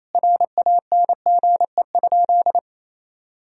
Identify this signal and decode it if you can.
Morse code, heard as a keyed tone, about 28 wpm, 700 Hz. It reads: RANGE?
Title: Sentences from Top 1000 Words - Review Part 4 - 28wpm